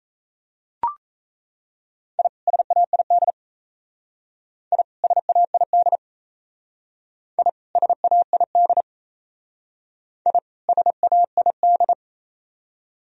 Morse code, heard as a keyed tone, about 37 wpm, 690 Hz. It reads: I SAID I SAID I SAID I SAID